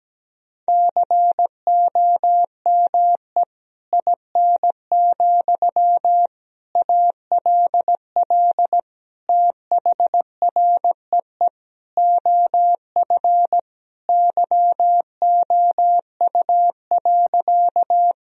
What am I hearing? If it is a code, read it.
Morse code, heard as a keyed tone, about 17 wpm, 700 Hz. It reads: COME IN, ALL THREE OF YOU.
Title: Sentences from Top 1000 Words - Review Part 1 - 17wpm